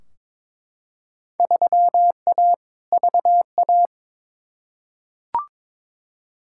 Morse code, heard as a keyed tone, about 22 wpm, 700 Hz. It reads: 3A VA
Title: Field Day Exchanges - 22wpm